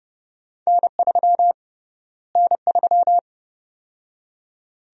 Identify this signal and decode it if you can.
Morse code, heard as a keyed tone, about 30 wpm, 700 Hz. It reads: N3 N3
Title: USA Call Sign Prefixes - Repeated 2x - 30wpm